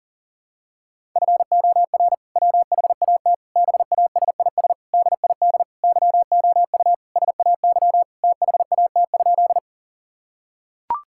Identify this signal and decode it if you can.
Morse code, heard as a keyed tone, about 37 wpm, 700 Hz. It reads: FOR WHAT BASIS DID YOU SAY THAT?
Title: Sentences from Top 1000 Words - Review Part 5 - 40wpm